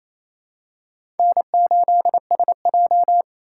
Morse code, heard as a keyed tone, about 28 wpm, 700 Hz. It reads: N8SJ
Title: USA Call Signs - 4 Characters - 4x Character Spacing - 28wpm